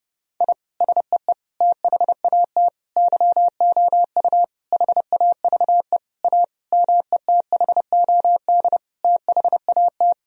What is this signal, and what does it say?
Morse code, heard as a keyed tone, about 30 wpm, 700 Hz. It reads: I SEE THAT YOU HAVE A METHOD THAT
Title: Sentences from Top 1000 Words - Review Part 3 - 30wpm